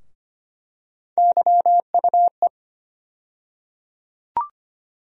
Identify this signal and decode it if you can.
Morse code, heard as a keyed tone, about 25 wpm, 700 Hz. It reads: YUE
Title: USA Call Sign Postfixes - 3 Letters 25wpm